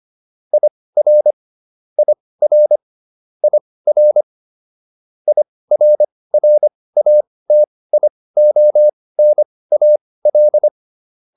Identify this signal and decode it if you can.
Morse code, heard as a keyed tone, about 25 wpm, spaced as for 18 wpm, 600 Hz. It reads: IR IR IR IRRATIONAL